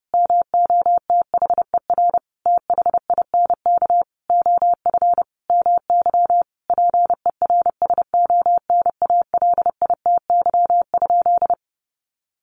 Morse code, heard as a keyed tone, about 30 wpm, 700 Hz. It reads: MOTHER THINK OF MY PERSONALITY?
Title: Sentences from Top 1400 Words 30wpm